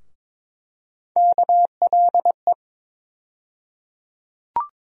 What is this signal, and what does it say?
Morse code, heard as a keyed tone, about 22 wpm, 700 Hz. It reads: KLE
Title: USA Call Sign Postfixes - 3 Letters 22wpm